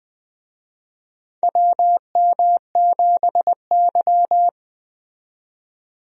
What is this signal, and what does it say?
Morse code, heard as a keyed tone, about 20 wpm, 700 Hz. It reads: WM7Y
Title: USA Call Signs - 4 Characters - 20wpm